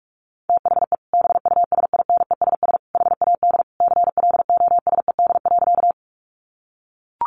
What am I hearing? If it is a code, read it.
Morse code, heard as a keyed tone, about 44 wpm, 700 Hz. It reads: THE BUSINESS HAD CLOSED.
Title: Sentences from Top 1400 Words 45wpm